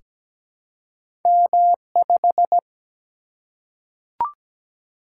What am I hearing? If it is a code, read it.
Morse code, heard as a keyed tone, about 17 wpm, 700 Hz. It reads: M5
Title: Single-Letter Single-Number 17wpm